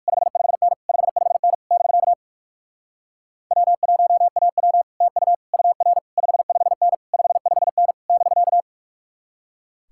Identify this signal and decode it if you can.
Morse code, heard as a keyed tone, about 58 wpm, 700 Hz. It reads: 55N 55N <BK> W1AW TU UR 55N 55N <BK>